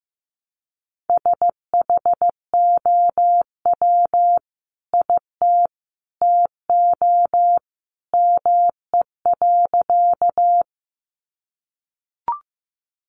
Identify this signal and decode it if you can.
Morse code, heard as a keyed tone, about 15 wpm, 700 Hz. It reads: SHOW IT TO ME.
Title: Sentences from Top 200 Words 15wpm